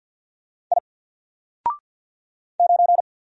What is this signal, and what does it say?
Morse code, heard as a keyed tone, about 49 wpm, 700 Hz. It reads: I 9
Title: Single Letter-Number 50wpm